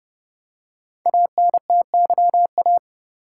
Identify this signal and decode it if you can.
Morse code, heard as a keyed tone, about 30 wpm, 700 Hz. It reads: ANTYA
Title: Code Groups - 5 Letter:Numbers - 30wpm